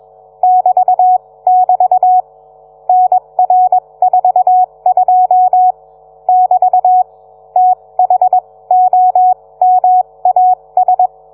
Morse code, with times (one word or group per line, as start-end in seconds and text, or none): 0.4-2.2 <BT><BT>
2.9-5.7 NR42
6.3-7.0 <BT>
7.6-11.1 THOMAS